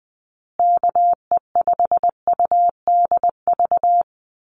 Morse code, heard as a keyed tone, about 20 wpm, 700 Hz. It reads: KE5UDV
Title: USA Call Signs Repeated 2x 20wpm